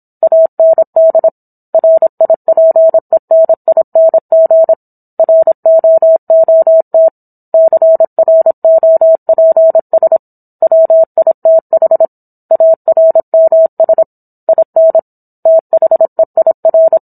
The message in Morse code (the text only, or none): AND RIPENING ROOT CROPS WITH ARMS IN THEIR